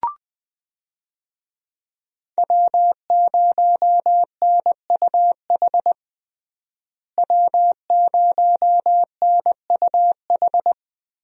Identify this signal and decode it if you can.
Morse code, heard as a keyed tone, about 21 wpm, 700 Hz. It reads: W0NUH W0NUH
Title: USA Call Signs - 5 Characters - Repeated 2x - 20wpm